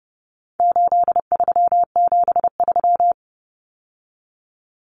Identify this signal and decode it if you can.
Morse code, heard as a keyed tone, about 30 wpm, 700 Hz. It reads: 8373
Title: Numbers - 4-Digit 30wpm